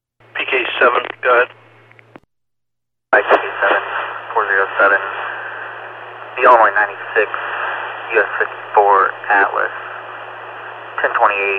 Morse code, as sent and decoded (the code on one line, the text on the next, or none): none